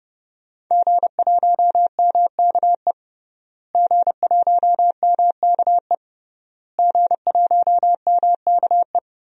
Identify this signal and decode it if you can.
Morse code, heard as a keyed tone, about 29 wpm, 700 Hz. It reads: G1MKE G1MKE G1MKE
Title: DXCC-100 Call Signs - 5 Characters - Repeated 3x - 30wpm